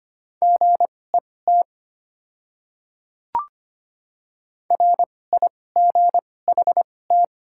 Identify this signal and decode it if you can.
Morse code, heard as a keyed tone, about 25 wpm, spaced as for 18 wpm, 700 Hz. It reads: GET RIGHT
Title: 2x Character Spacing - Top 300 Words - 25wpm